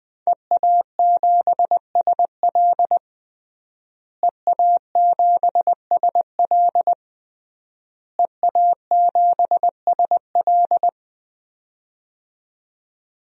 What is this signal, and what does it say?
Morse code, heard as a keyed tone, about 21 wpm, 700 Hz. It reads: EA7SL EA7SL EA7SL
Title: DXCC-100 Call Signs - 5 Characters - Repeated 3x - 20wpm